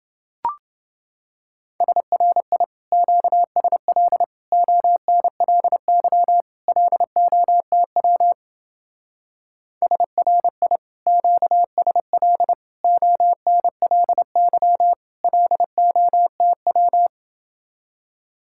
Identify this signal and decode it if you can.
Morse code, heard as a keyed tone, about 29 wpm, 700 Hz. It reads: SRI QSL ONLY LOTW SRI QSL ONLY LOTW